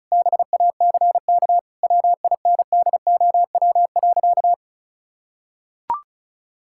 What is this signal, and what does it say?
Morse code, heard as a keyed tone, about 34 wpm, 700 Hz. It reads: BACK WINDOW.
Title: Sentences from Top 1600 Words 35wpm